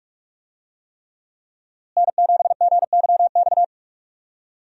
Morse code, heard as a keyed tone, about 46 wpm, 700 Hz. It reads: N7GYX
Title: USA Call Signs - 5 Characters - Repeated 3x - 45wpm